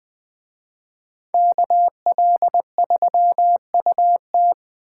Morse code, heard as a keyed tone, about 21 wpm, 700 Hz. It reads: KL3UT
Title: Speed Racing - USA Call Signs 17wpm